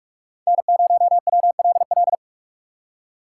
Morse code, heard as a keyed tone, about 46 wpm, 700 Hz. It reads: N0WLL
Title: USA Call Signs - 2x Character Spacing - 45wpm